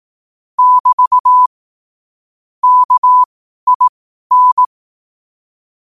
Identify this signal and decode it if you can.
Morse code, heard as a keyed tone, about 18 wpm, spaced as for 12 wpm, 1000 Hz. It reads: <BT> KIN
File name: qotdfiles / 12wpm